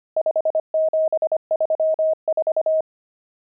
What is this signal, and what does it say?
Morse code, heard as a keyed tone, about 25 wpm, 630 Hz. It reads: 5734
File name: pangram-fs-25-cs-25-nums10